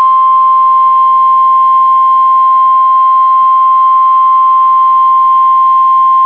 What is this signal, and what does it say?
Emergency Alert System (SAME) tones (the 1050 Hz attention tone). It is heard from 0.9 to 5.4 s.